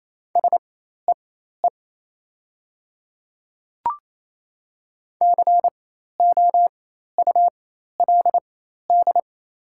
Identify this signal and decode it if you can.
Morse code, heard as a keyed tone, about 28 wpm, spaced as for 13 wpm, 700 Hz. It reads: SEE COULD
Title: 4x Character Spacing - Top 100 Words - 28wpm